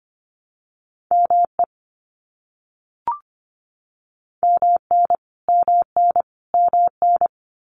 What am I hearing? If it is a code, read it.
Morse code, heard as a keyed tone, about 25 wpm, 700 Hz. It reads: ME MN MN MN
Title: US States Repeated 3x 25wpm